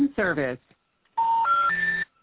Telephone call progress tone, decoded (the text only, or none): special information tone